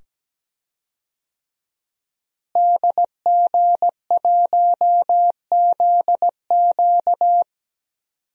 Morse code, heard as a keyed tone, about 17 wpm, 700 Hz. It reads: DG1ZQ
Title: DXCC-100 Call Signs - 5 Characters - Repeated 2x - 17wpm